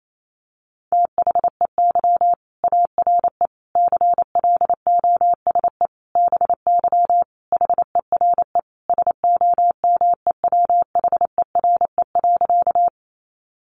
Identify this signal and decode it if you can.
Morse code, heard as a keyed tone, about 28 wpm, 700 Hz. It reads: THEY ARE CLOSE BY HERE SOMEWHERE.
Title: Sentences from Top 1400 Words 28wpm